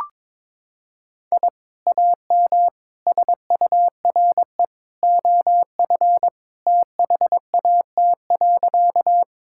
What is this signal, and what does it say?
Morse code, heard as a keyed tone, about 22 wpm, 700 Hz. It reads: I AM SURE OF THAT.
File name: practice / 22wpm / Sentences with 2-Word Phrases - 22wpm